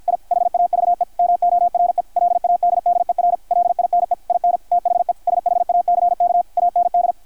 Morse code, heard as a keyed tone, about 53 wpm, 720 Hz, with radio noise. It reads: I HAVE MORE LADDER LINE IN THE SHACK AND
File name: archived / CRSnet